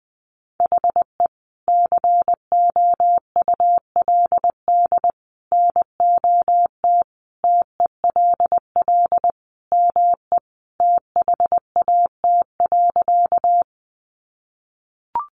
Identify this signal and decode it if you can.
Morse code, heard as a keyed tone, about 19 wpm, 700 Hz. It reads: HE COULD NOT TELL ME THAT.
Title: Sentences from Top 600 Words 20wpm